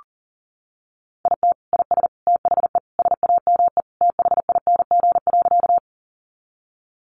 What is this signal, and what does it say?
Morse code, heard as a keyed tone, about 40 wpm, 700 Hz. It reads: IT IS THE SAME THING.
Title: Sentences from Top 200 Words 40wpm